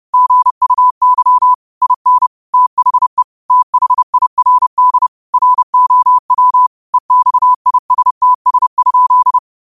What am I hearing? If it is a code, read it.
Morse code, heard as a keyed tone, about 30 wpm, 1000 Hz. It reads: GUY IN THE THIRD ROW EXISTS?